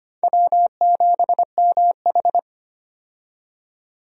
Morse code, heard as a keyed tone, about 25 wpm, 700 Hz. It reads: W7MH
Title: USA Call Signs - 4 Characters - 25wpm